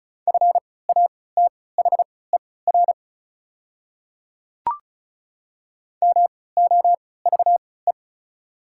Morse code, heard as a keyed tone, about 36 wpm, spaced as for 20 wpm, 700 Hz. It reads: FATHER MOVE
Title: 3x Character Spacing - Top 200 Words - 35wpm